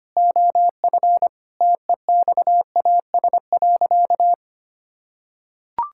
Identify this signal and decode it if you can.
Morse code, heard as a keyed tone, about 25 wpm, 700 Hz. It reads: OF TEXAS.
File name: practice / 25wpm / Sentences with 4-Word Phrases - 25wpm